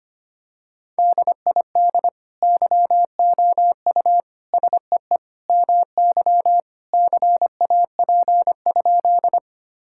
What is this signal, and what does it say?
Morse code, heard as a keyed tone, about 25 wpm, 700 Hz. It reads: DID YOU SEE MY CAP?
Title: Sentences from Top 1600 Words 25wpm